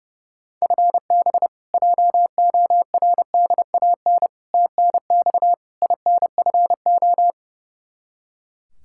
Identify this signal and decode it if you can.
Morse code, heard as a keyed tone, about 32 wpm, 700 Hz. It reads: FB JORDAN TNX INFO
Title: Practice QSO - Exchange 3 - More Info - 30wpm